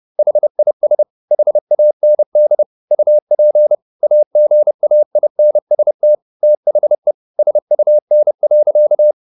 Morse code, heard as a keyed tone, about 30 wpm, 600 Hz. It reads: HIS HAND UP AGAINST THE SUN.
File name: the old man and the sea / the old man and the sea 1 30wpm